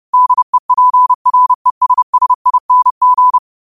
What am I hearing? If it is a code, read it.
Morse code, heard as a keyed tone, about 30 wpm, 1000 Hz. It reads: DEPRESSING